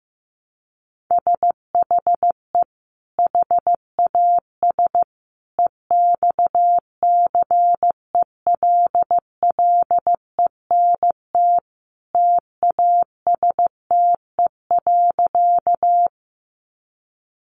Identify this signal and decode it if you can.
Morse code, heard as a keyed tone, about 15 wpm, 700 Hz. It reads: SHE HAS EXCELLENT TASTE.
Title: Sentences from Top 1100 Words 15wpm